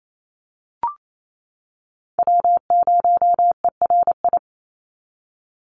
Morse code, heard as a keyed tone, about 28 wpm, 700 Hz. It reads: W0ERI
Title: NA SOTA Chaser Callsigns 28wpm